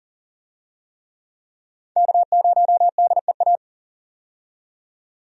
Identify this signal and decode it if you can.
Morse code, heard as a keyed tone, about 40 wpm, 700 Hz. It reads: K0DEA
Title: USA Call Signs - 5 Characters - 40wpm